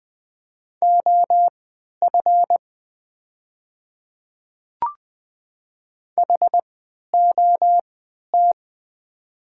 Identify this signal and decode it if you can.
Morse code, heard as a keyed tone, about 20 wpm, spaced as for 11 wpm, 700 Hz. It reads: OF HOT
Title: 3x Character Spacing - Top 200 Words - 20wpm